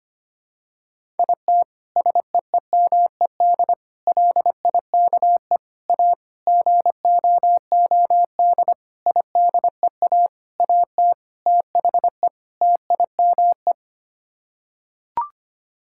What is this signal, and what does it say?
Morse code, heard as a keyed tone, about 25 wpm, 700 Hz. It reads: IT SEEMED LIKE A GOOD IDEA AT THE TIME
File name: Sets of 9 Words - 25wpm